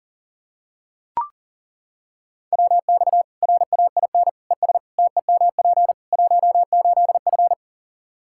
Morse code, heard as a keyed tone, about 40 wpm, 700 Hz. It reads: WX RAIN ES TEMP 18F
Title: K1USN SST QSO Practice 40wpm